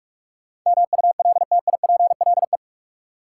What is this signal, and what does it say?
Morse code, heard as a keyed tone, about 41 wpm, 700 Hz. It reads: MULTIPLE
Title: Sets of 5 Words - New Word Drill - 45wpm